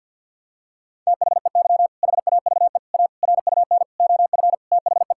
Morse code, heard as a keyed tone, about 52 wpm, 690 Hz. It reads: THEY HAVE A RUN OF THE
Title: Sentences from Top 1000 Words - Review Part 5 - 50wpm